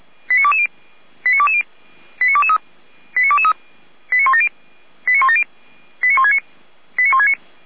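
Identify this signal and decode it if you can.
Five-tone selective call (ZVEI), heard as quick runs of five tones, tones of about 70 ms. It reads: 89220 89220 89223 89223 89170 89170 89169 89169
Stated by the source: Five-tone ZVEI selective call received by SDR.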